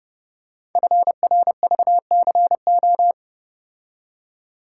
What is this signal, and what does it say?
Morse code, heard as a keyed tone, about 30 wpm, 700 Hz. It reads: FRVCO
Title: Code Groups - 5 Letters - 30wpm